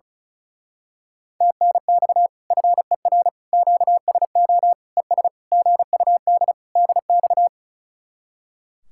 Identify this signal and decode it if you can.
Morse code, heard as a keyed tone, about 33 wpm, 700 Hz. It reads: TNX FER QSO ES GUD DX
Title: Speed-Racing Common QSO Phrases - 35wpm